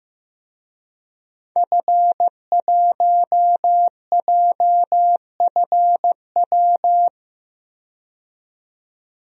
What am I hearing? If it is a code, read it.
Morse code, heard as a keyed tone, about 15 wpm, 700 Hz. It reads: F1JFW
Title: DXCC-100 Call Signs - 5 Characters - 15wpm